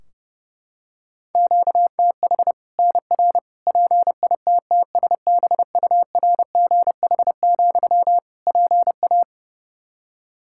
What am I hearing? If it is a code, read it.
Morse code, heard as a keyed tone, about 30 wpm, 700 Hz. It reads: QTH NR PITTSBURGH, PA
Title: Practice QSO - Exchange 1 - Beginning - 30wpm